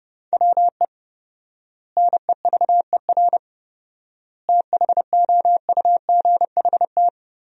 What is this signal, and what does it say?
Morse code, heard as a keyed tone, about 29 wpm, 700 Hz. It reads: WE NEVER THOUGHT
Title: 4x Word-Spacing - Sentences from Top 300 Words - 30wpm